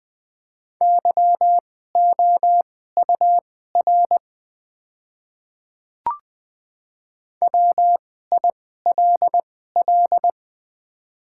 Morse code, heard as a keyed tone, about 20 wpm, spaced as for 14 wpm, 700 Hz. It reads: YOUR WILL
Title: 2x Character Spacing - Top 200 Words - 20wpm